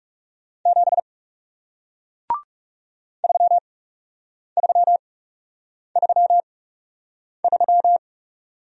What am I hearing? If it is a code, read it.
Morse code, heard as a keyed tone, about 40 wpm, spaced as for 8 wpm, 700 Hz. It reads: 7 3333